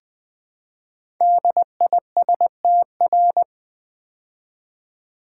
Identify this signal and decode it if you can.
Morse code, heard as a keyed tone, about 20 wpm, 700 Hz. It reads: DISTR
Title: Build A Word - 5 Letters At A Time - 12-Letter Words - 20wpm